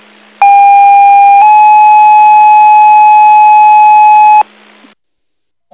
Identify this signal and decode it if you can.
Two-tone page, heard as a one-second tone, then a three-second tone, about 800 then 850 Hz. It is heard from 0.4 to 4.4 s.